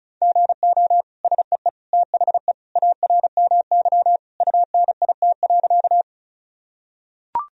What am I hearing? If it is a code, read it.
Morse code, heard as a keyed tone, about 36 wpm, 700 Hz. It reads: GO SEE THE ARMY UNIT.
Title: Sentences from Top 1000 Words - Review Part 2 - 35wpm